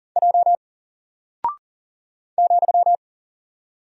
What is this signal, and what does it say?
Morse code, heard as a keyed tone, about 40 wpm, 700 Hz. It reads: J ,
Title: Single Character (Letter:Number:Punctuation:Prosign) 40wpm